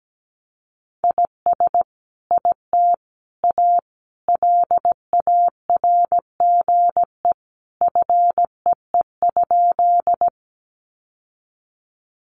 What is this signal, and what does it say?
Morse code, heard as a keyed tone, about 17 wpm, 700 Hz. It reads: IS IT A LARGE FEE?
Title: Sentences from Top 1500 Words 17wpm